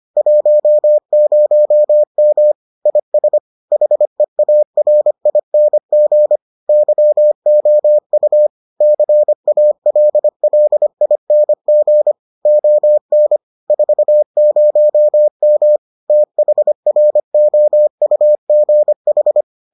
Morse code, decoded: 10M IS HEARING YOU CALLING ON 40M THROUGH

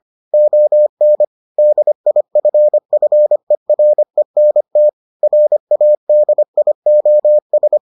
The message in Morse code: ON DIFFERENT RADIOS

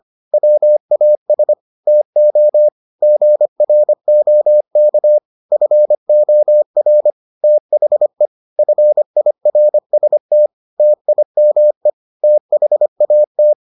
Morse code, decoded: WAS TO GROK FOR THE FIRST TIME THAT

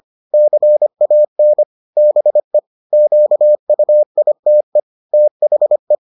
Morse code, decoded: CAN BE QUITE THE

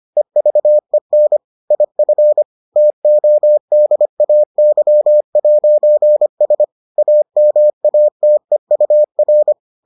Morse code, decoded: EVEN IF TODAY'S AMATEUR

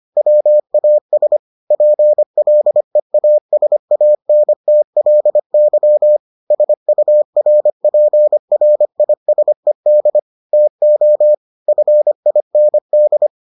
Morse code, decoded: WAS PLEASANTLY SURPRISED TO FIND

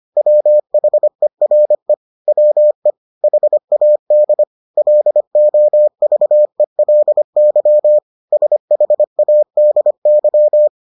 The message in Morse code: WHERE WE HAD LOVELY SHADY

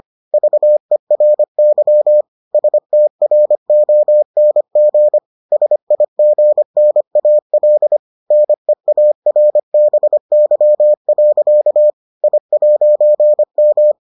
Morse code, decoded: VERY STRONG SIGNAL NEARBY. I'M

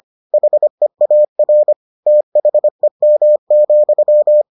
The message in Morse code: HEAR THEM,

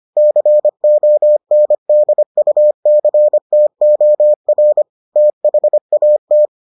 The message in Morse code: CONDUCTOR THAT

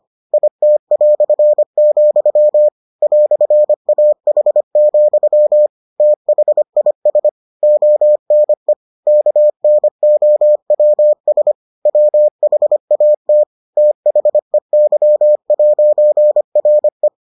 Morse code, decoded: IT", "AH, THIS ONE KNOWS WHAT THEY'RE

